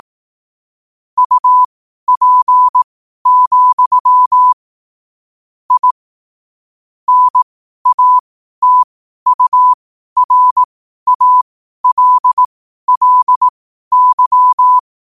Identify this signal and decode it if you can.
Morse code, heard as a keyed tone, about 18 wpm, spaced as for 12 wpm, 1000 Hz. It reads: UP, I NATURALLY
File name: qotdfiles / 12wpm